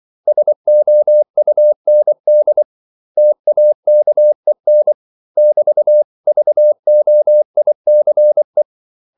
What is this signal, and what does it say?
Morse code, heard as a keyed tone, about 24 wpm, 600 Hz. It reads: SOUND TAKEN <BT>VOICE
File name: CW / Practice / 500MostCommonEnglishWords24